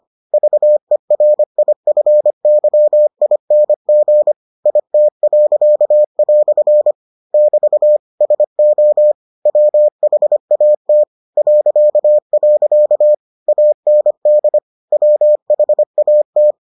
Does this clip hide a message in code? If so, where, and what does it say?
Yes: Morse code from 0.3 to 16.5 s, heard as a keyed tone, about 25 wpm, 600 Hz. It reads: VERIFYING IT." <BT> SO WHAT .. AND WHAT